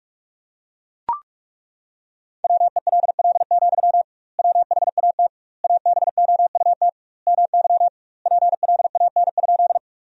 Morse code, heard as a keyed tone, about 41 wpm, 700 Hz. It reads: WELL, WHAT ABOUT MY PLAN?
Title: 2x Word-Spacing - Sentences from Top 500 Words - 45wpm